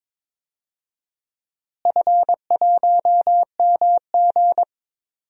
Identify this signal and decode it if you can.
Morse code, heard as a keyed tone, about 22 wpm, 700 Hz. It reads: F1MG
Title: DXCC-100 Call Signs - 4 Characters - Repeated 3x - 22wpm